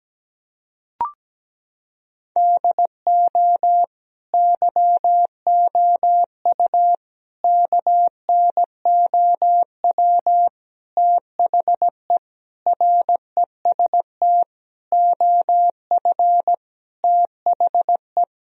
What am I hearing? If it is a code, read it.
Morse code, heard as a keyed tone, about 17 wpm, 700 Hz. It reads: DO YOU KNOW THE REST OF THE
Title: Sentences from Top 1000 Words - Review Part 2 - 17wpm